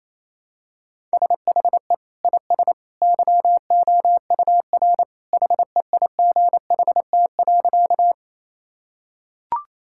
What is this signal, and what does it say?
Morse code, heard as a keyed tone, about 28 wpm, 700 Hz. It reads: SHE IS YOUR HEIGHT.
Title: Sentences from Top 1600 Words 28wpm